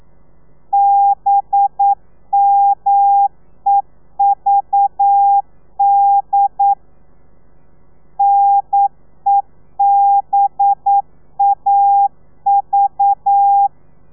Morse code, heard as a keyed tone, about 9 wpm, 800 Hz. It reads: BMEVD NEBAV
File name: telegrafia / i2tfj / Lez4 40